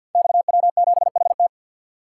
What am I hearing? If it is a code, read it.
Morse code, heard as a keyed tone, about 52 wpm, 700 Hz. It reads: XW7HT
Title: Code Groups - 5 Letter:Numbers - 50wpm